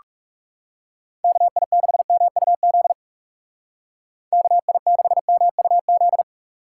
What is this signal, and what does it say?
Morse code, heard as a keyed tone, about 39 wpm, 700 Hz. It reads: KI6MUZ KI6MUZ